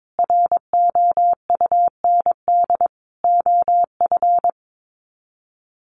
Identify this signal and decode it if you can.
Morse code, heard as a keyed tone, about 22 wpm, 700 Hz. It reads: ROUND OF
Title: Sets of 5 Words Encore 22wpm